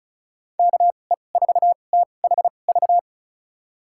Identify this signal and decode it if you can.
Morse code, heard as a keyed tone, about 36 wpm, spaced as for 25 wpm, 700 Hz. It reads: KE4THV
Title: USA Call Signs - 6 Characters - 2x Character Spacing - 35wpm